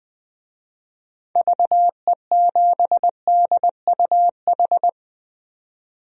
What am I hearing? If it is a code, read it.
Morse code, heard as a keyed tone, about 21 wpm, 700 Hz. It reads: VE7DUH
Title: DXCC-100 Call Signs - 6 Characters - 20wpm